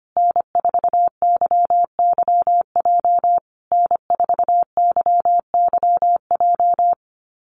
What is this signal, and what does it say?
Morse code, heard as a keyed tone, about 25 wpm, 700 Hz. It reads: N4YYJ N4YYJ
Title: USA Call Signs Repeated 3x 25wpm